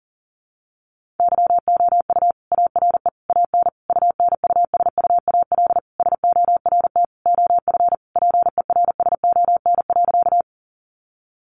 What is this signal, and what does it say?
Morse code, heard as a keyed tone, about 37 wpm, 700 Hz. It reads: YOU ARE AN UNUSUAL SORT OF PERSON.
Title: Sentences from Top 1400 Words 40wpm